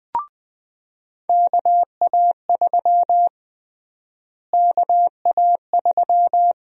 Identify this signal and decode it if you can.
Morse code, heard as a keyed tone, about 20 wpm, 700 Hz. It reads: KA3 KA3
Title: USA Call Sign Prefixes - Repeated 2x - 20wpm